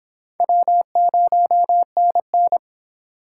Morse code, heard as a keyed tone, about 26 wpm, 700 Hz. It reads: W0NN